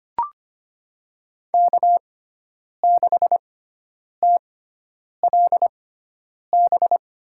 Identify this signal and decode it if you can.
Morse code, heard as a keyed tone, about 25 wpm, spaced as for 9 wpm, 700 Hz. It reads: K6TLB
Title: USA Call Signs - 5 Characters - 6x Character Spacing - 25wpm